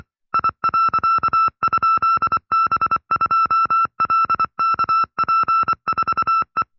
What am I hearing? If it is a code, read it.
Morse code, heard as a keyed tone, about 24 wpm, 1330 Hz. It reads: I.?B2LKP4E
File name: FreeFiles / CW-practice / 20WPM-side2